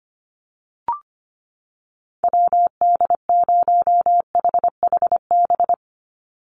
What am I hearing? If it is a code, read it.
Morse code, heard as a keyed tone, about 25 wpm, 700 Hz. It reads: WD0HHB
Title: USA Call Signs 25wpm